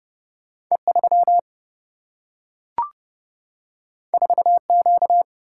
Morse code, heard as a keyed tone, about 30 wpm, 700 Hz. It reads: E3 4Q
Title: Single-Letter Single-Number 30wpm